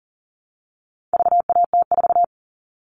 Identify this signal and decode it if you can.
Morse code, heard as a keyed tone, about 37 wpm, 700 Hz. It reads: VAT4
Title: All Cut Numbers - 4-Digit 40wpm